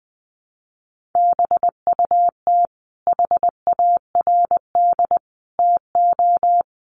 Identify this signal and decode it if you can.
Morse code, heard as a keyed tone, about 20 wpm, 700 Hz. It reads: BUT HARD TO
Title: Sentences from Top 300 Words 20wpm